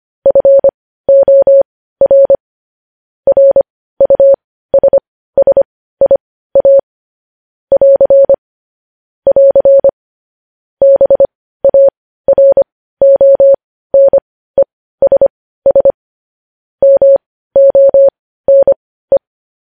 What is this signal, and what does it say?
Morse code, heard as a keyed tone, about 25 wpm, spaced as for 15 wpm, 550 Hz. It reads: FOR RUSSIA <AR> <AR> BARONESS MONE
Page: news-headlines-in-morse-code-at-15-wpm